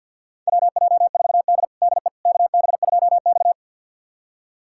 Morse code, heard as a keyed tone, about 46 wpm, 700 Hz. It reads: WJ4D DE KB2X
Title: USA Callsign DE Callsign - 4 Characters - 50wpm